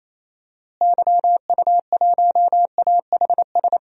Morse code, heard as a keyed tone, about 28 wpm, 700 Hz. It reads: YU1AHS
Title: DXCC-100 Call Signs - 6 Characters - Repeated 3x - 28wpm